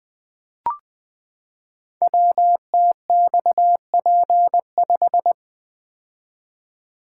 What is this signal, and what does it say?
Morse code, heard as a keyed tone, about 20 wpm, 700 Hz. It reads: WTXP5
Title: Code Groups - 5 Letter:Numbers - 20wpm